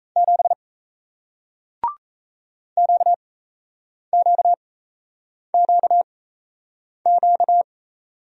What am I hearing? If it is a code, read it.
Morse code, heard as a keyed tone, about 38 wpm, spaced as for 8 wpm, 700 Hz. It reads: 7 QQQQ